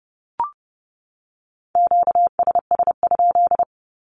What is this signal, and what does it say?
Morse code, heard as a keyed tone, about 30 wpm, 700 Hz. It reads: QSS?
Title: Q-Codes 30wpm